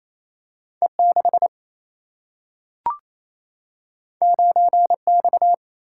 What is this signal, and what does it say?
Morse code, heard as a keyed tone, about 28 wpm, 700 Hz. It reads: E6 9X
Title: Single-Letter Single-Number 28wpm